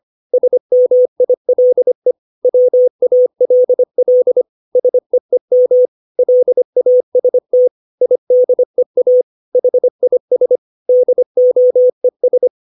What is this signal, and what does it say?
Morse code, heard as a keyed tone, about 25 wpm, 500 Hz. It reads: SMILE WALL SEEM LAST IDEA HIS DOES
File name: 632 words / 632 Words 25WPM